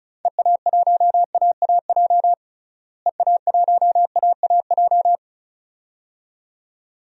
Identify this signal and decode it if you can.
Morse code, heard as a keyed tone, about 34 wpm, 700 Hz. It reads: EA1AAJ EA1AAJ
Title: DXCC-100 Call Signs - 6 Characters - Repeated 2x - 35wpm